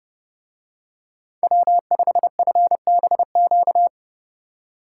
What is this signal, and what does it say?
Morse code, heard as a keyed tone, about 30 wpm, 700 Hz. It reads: W5FBQ